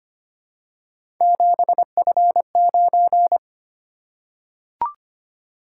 Morse code, heard as a keyed tone, about 25 wpm, 700 Hz. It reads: 7F9
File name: Three Letter-Number Combinations - 25wpm